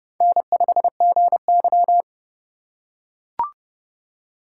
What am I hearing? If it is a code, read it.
Morse code, heard as a keyed tone, about 29 wpm, 700 Hz. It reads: N5GY